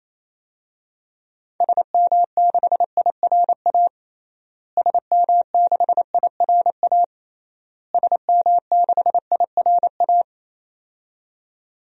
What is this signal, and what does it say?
Morse code, heard as a keyed tone, about 28 wpm, 700 Hz. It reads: SM6IRA SM6IRA SM6IRA
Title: DXCC-100 Call Signs - 6 Characters - Repeated 3x - 28wpm